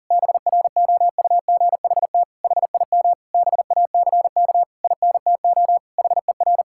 Morse code, heard as a keyed tone, about 37 wpm, 700 Hz. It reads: BROUGHT HIM BACK INTO HER
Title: Sentences from Top 1000 Words - Review Part 6 - 40wpm